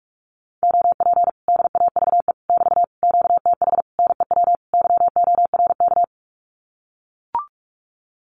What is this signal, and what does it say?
Morse code, heard as a keyed tone, about 44 wpm, 700 Hz. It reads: OP DAVE <BT> QTH NEW YORK
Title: Common QSO Phrases Encore 45wpm